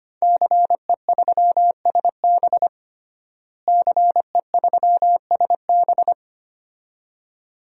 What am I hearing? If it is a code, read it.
Morse code, heard as a keyed tone, about 25 wpm, 700 Hz. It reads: CE3SB CE3SB
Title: DXCC-100 Call Signs - 5 Characters - Repeated 2x - 25wpm